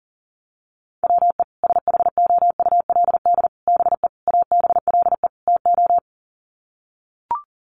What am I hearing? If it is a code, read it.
Morse code, heard as a keyed tone, about 40 wpm, 700 Hz. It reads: WE SHOULD BE ABLE TO
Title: Sets of 5 Words Encore2 40wpm